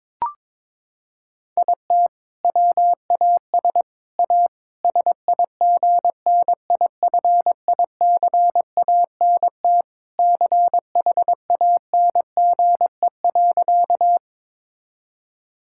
Morse code, heard as a keyed tone, about 21 wpm, 700 Hz. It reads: IT WAS A SIGNIFICANT CHANGE.
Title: Sentences from Top 1000 Words - Review Part 6 - 22wpm